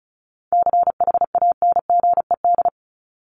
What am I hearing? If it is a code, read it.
Morse code, heard as a keyed tone, about 34 wpm, 700 Hz. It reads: CHANGED
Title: ED Suffix - Top 100 Words with Base Words 35wpm